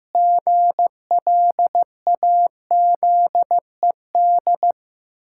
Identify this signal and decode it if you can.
Morse code, heard as a keyed tone, about 16 wpm, 700 Hz. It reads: GLAZED
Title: Pangram Challenge v2 - 15wpm